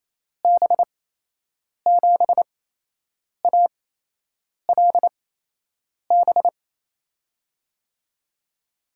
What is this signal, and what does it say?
Morse code, heard as a keyed tone, about 28 wpm, spaced as for 8 wpm, 700 Hz. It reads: B7ALB
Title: USA Call Signs - 6 Characters - 8x Character Spacing - 28wpm